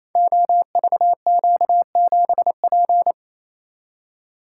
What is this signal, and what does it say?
Morse code, heard as a keyed tone, about 27 wpm, 700 Hz. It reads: OVQ7P
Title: Code Groups - 5 Letter:Numbers - 28wpm